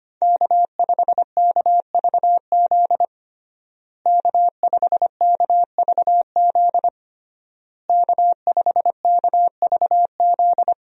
Morse code, heard as a keyed tone, about 25 wpm, 700 Hz. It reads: K5KVZ K5KVZ K5KVZ
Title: USA Call Signs - 5 Characters - Repeated 3x - 25wpm